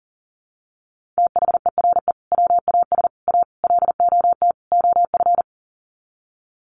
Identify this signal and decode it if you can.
Morse code, heard as a keyed tone, about 40 wpm, 700 Hz. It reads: THERE WAS A LOT OF
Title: Sets of 5 Words 40wpm